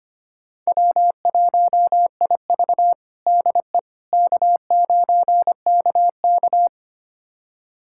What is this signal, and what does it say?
Morse code, heard as a keyed tone, about 25 wpm, 700 Hz. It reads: W1IV DE K9KK
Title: USA Callsign DE Callsign - 4 Characters - 25wpm